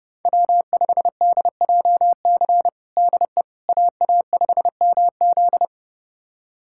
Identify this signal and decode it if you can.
Morse code, heard as a keyed tone, about 30 wpm, 700 Hz. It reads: W5DJC DE AA5MZ
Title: USA Callsign DE Callsign - 5 Characters - 30wpm